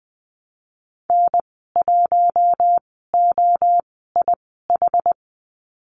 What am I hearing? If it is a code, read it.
Morse code, heard as a keyed tone, about 19 wpm, spaced as for 14 wpm, 700 Hz. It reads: N1OIH
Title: 062 Lesson 11g - Letter h - Call Signs 20wpm